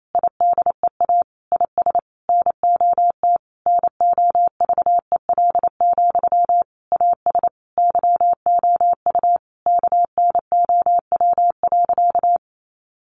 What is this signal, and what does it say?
Morse code, heard as a keyed tone, about 28 wpm, 700 Hz. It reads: IDEA IS NOT NOVEL, AS YOU KNOW.
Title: Sentences from Top 1300 Words 28wpm